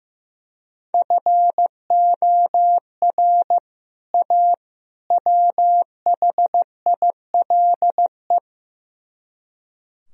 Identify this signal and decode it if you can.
Morse code, heard as a keyed tone, about 15 wpm, 700 Hz. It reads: FOR A WHILE
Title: Speed-Racing - Sets of 3 Words 15wpm